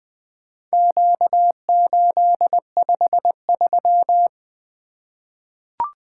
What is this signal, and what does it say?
Morse code, heard as a keyed tone, about 20 wpm, 700 Hz. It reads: Q853
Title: Four Letter-Number Combinations - 20wpm